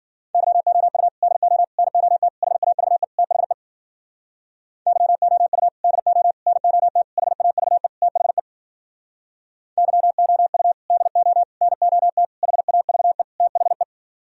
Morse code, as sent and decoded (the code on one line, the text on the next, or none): -.-- --- ..- / -.. --- / -. --- - / .... .- ...- . / - .... / -.-- --- ..- / -.. --- / -. --- - / .... .- ...- . / - .... / -.-- --- ..- / -.. --- / -. --- - / .... .- ...- . / - .... .
YOU DO NOT HAVE THE YOU DO NOT HAVE THE YOU DO NOT HAVE THE